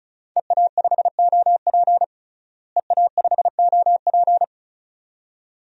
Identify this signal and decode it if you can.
Morse code, heard as a keyed tone, about 34 wpm, 700 Hz. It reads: EA5OP EA5OP
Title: DXCC-100 Call Signs - 5 Characters - Repeated 2x - 35wpm